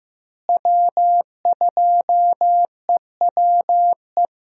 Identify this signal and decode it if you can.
Morse code, heard as a keyed tone, about 14 wpm, 700 Hz. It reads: W2EWE
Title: USA Call Signs - 5 Characters - Repeated 3x - 15wpm